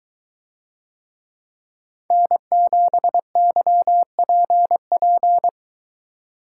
Morse code, heard as a keyed tone, about 22 wpm, 700 Hz. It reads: N7YPP